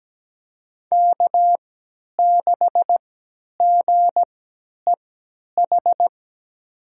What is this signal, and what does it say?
Morse code, heard as a keyed tone, about 17 wpm, spaced as for 10 wpm, 700 Hz. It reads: K6GEH
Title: USA Call Signs - 3x Character Spacing - 17wpm